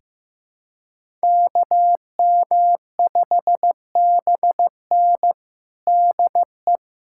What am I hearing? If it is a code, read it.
Morse code, heard as a keyed tone, about 16 wpm, 700 Hz. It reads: KM5BN DE
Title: USA Callsign DE Callsign - 5 Characters - 15wpm